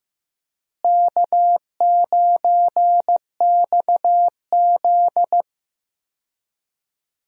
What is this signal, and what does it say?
Morse code, heard as a keyed tone, about 15 wpm, 700 Hz. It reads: K9XZ
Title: USA Call Signs - 4 Characters - Repeated 3x - 15wpm